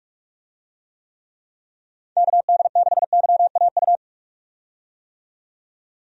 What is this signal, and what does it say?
Morse code, heard as a keyed tone, about 46 wpm, 700 Hz. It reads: KD6YAU
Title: USA Call Signs - 6 Characters - 45wpm